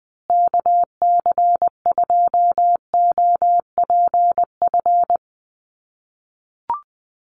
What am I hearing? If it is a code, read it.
Morse code, heard as a keyed tone, about 20 wpm, 700 Hz. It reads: KC2OPF
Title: USA Call Signs Repeated 2x 20wpm